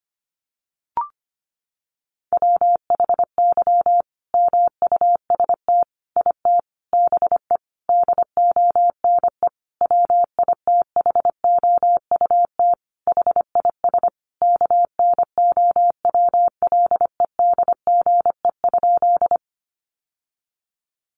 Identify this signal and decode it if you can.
Morse code, heard as a keyed tone, about 25 wpm, 700 Hz. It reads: WHY MUST IT BE DONE WITHOUT HIS KNOWLEDGE?
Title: Sentences from Top 600 Words 25wpm